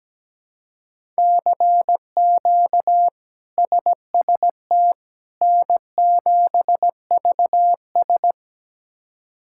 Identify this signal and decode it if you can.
Morse code, heard as a keyed tone, about 17 wpm, 700 Hz. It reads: CQ SST N7VS
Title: K1USN SST QSO Practice - 17wpm